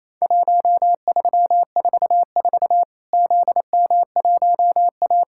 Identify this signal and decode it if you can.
Morse code, heard as a keyed tone, about 28 wpm, 700 Hz. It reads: 1344 ZM1A
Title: Worked All Europe DX Contest - QTC Exchange - 28wpm